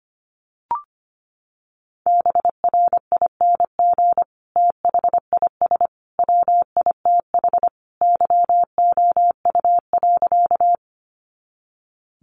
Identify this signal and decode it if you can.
Morse code, heard as a keyed tone, about 25 wpm, 700 Hz. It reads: BRING THIS WITH YOU.
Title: Sentences from Top 500 Words 25wpm